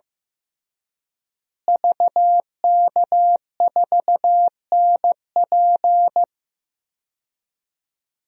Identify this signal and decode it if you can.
Morse code, heard as a keyed tone, about 16 wpm, 700 Hz. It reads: VK4NP
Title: DXCC-100 Call Signs - 5 Characters - Repeated 3x - 15wpm